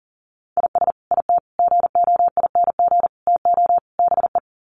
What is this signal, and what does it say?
Morse code, heard as a keyed tone, about 40 wpm, 700 Hz. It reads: IS IT GOING TO BE
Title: Sets of 5 Words Encore2 40wpm